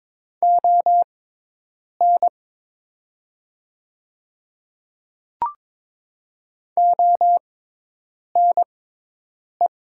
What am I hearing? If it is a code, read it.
Morse code, heard as a keyed tone, about 22 wpm, spaced as for 8 wpm, 700 Hz. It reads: ON ONE
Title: 6x Character Spacing - Top 100 Words - 22wpm